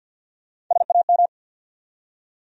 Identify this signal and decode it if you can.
Morse code, heard as a keyed tone, about 51 wpm, 700 Hz. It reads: SAM
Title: USA Common Names - Max 3 Letters - 50wpm